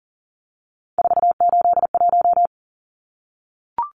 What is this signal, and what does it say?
Morse code, heard as a keyed tone, about 40 wpm, 700 Hz. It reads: 481
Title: Numbers - 3-Digit 40wpm